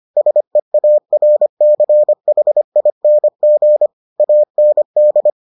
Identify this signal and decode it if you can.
Morse code, heard as a keyed tone, about 25 wpm, 600 Hz. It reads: SEARCHING AND